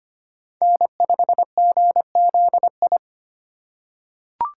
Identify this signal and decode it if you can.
Morse code, heard as a keyed tone, about 25 wpm, 700 Hz. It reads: N5GZI
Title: USA Call Signs - 5 Characters - 4x Character Spacing - 25wpm